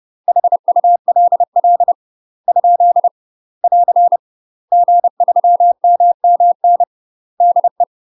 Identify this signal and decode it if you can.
Morse code, heard as a keyed tone, about 30 wpm, 700 Hz. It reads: HULL ? <AR> G3MMN DE
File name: UK MASTER QSOs TRAINING / 30WPM / UK MASTER QSOs 0009